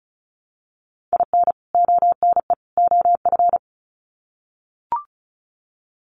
Morse code, heard as a keyed tone, about 34 wpm, 700 Hz. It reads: IN ONE OF
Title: Sets of 3 Words 35wpm